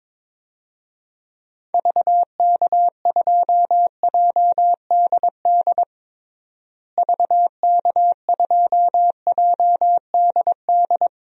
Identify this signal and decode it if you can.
Morse code, heard as a keyed tone, about 22 wpm, 700 Hz. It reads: VK2JDD VK2JDD
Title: DXCC-100 Call Signs - 6 Characters - Repeated 2x - 22wpm